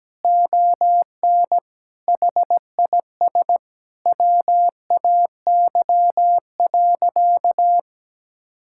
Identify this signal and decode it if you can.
Morse code, heard as a keyed tone, about 17 wpm, 700 Hz. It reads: ON HIS WAY.
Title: Sentences from Top 1000 Words - Review Part 5 - 17wpm